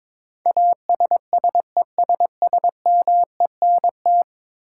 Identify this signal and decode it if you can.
Morse code, heard as a keyed tone, about 22 wpm, 700 Hz. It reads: ASSESSMENT
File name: practice / 22wpm / 10 Letter Words - First 3 Letters Given - 22wpm